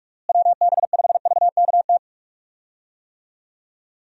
Morse code, heard as a keyed tone, about 46 wpm, 700 Hz. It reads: WB5VKT
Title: USA Call Signs - 6 Characters - Repeated 3x - 45wpm